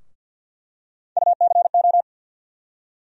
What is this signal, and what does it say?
Morse code, heard as a keyed tone, about 52 wpm, 700 Hz. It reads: UCO
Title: USA Call Sign Postfixes - 3 Letters 50wpm